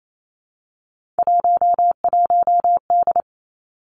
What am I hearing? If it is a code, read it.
Morse code, heard as a keyed tone, about 28 wpm, 700 Hz. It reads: 11D
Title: Three Challenging Letter-Number Combinations 28wpm